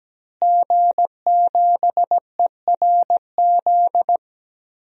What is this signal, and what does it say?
Morse code, heard as a keyed tone, about 17 wpm, 700 Hz. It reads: G7ERZ
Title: DXCC-100 Call Signs - 5 Characters - Repeated 3x - 17wpm